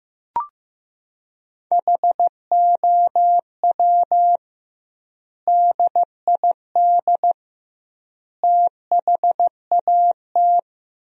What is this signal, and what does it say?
Morse code, heard as a keyed tone, about 15 wpm, 700 Hz. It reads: HOW DID THAT
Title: 2x Word-Spacing - Sentences from Top 200 Words - 15wpm